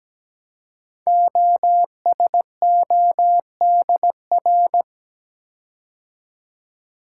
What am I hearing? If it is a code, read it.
Morse code, heard as a keyed tone, about 17 wpm, 700 Hz. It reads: OSODR
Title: Code Groups - 5 Letters - 17wpm